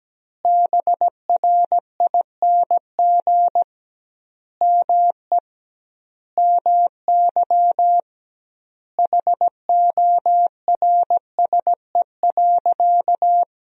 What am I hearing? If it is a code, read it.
Morse code, heard as a keyed tone, about 17 wpm, 700 Hz. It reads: BRING ME MY HORSE.
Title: 2x Word-Spacing - Sentences from Top 500 Words - 17wpm